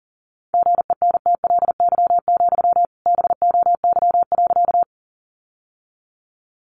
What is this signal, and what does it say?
Morse code, heard as a keyed tone, about 40 wpm, 700 Hz. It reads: GENTLY, BOY.
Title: Sentences from Top 1500 Words 40wpm